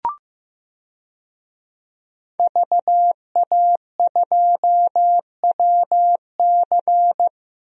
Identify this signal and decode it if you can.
Morse code, heard as a keyed tone, about 15 wpm, 700 Hz. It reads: VA2WC
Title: DXCC-100 Call Signs - 5 Characters - Repeated 3x - 15wpm